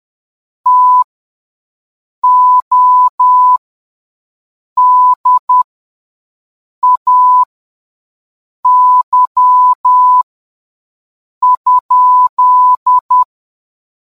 Morse code, heard as a keyed tone, about 10 wpm, spaced as for 5 wpm, 1000 Hz. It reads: TODAY?